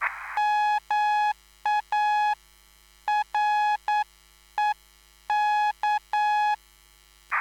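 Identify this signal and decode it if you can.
Morse code, heard as a keyed tone, about 9 wpm, spaced as for 8 wpm, 840 Hz. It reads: MAREK
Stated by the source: Live CW QSO between PA2BX and SP3JFK on the 14055 kHz QRS frequency, at a basic 8 wpm.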